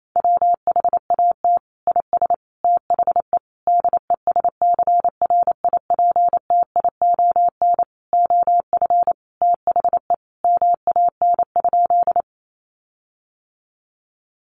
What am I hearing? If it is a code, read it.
Morse code, heard as a keyed tone, about 28 wpm, 700 Hz. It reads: WHAT IS THE DESCRIPTION OF THE MAN?
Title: Sentences from Top 1300 Words 28wpm